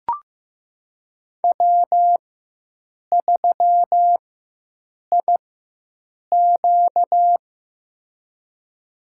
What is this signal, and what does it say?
Morse code, heard as a keyed tone, about 15 wpm, spaced as for 7 wpm, 700 Hz. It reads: W3IQ
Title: USA Call Signs - 4 Characters - 4x Character Spacing - 15wpm